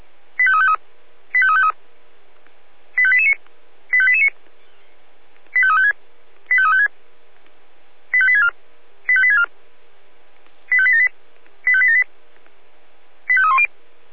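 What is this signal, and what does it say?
Five-tone selective call (ZVEI), heard as quick runs of five tones, tones of about 70 ms. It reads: 85353 85353 85889 85889 85356 85356 85864 85864 85878 85878 85310
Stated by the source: ZVEI five-tone selective call 85274 received via SDR.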